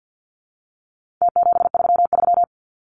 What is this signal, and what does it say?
Morse code, heard as a keyed tone, about 50 wpm, 700 Hz. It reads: T733
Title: Numbers - Contest Style - 4-Digit 50wpm